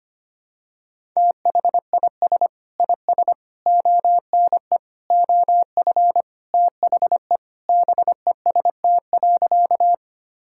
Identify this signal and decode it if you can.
Morse code, heard as a keyed tone, about 25 wpm, 700 Hz. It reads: THIS IS ONE OF THE BEST.
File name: practice / 25wpm / Sentences with 4-Word Phrases - 25wpm